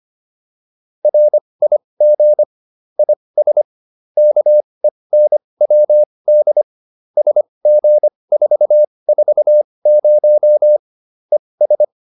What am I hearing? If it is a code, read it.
Morse code, heard as a keyed tone, about 25 wpm, spaced as for 20 wpm, 600 Hz. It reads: RIG IS KENWD SG440 ES
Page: intermediate-practice-files